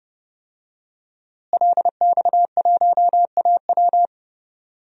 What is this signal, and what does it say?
Morse code, heard as a keyed tone, about 30 wpm, 700 Hz. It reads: LX1AW
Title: DXCC-100 Call Signs - 5 Characters - Repeated 3x - 30wpm